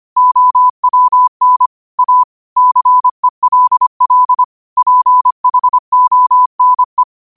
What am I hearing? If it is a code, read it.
Morse code, heard as a keyed tone, about 25 wpm, 1000 Hz. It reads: OWN A CELL PHONE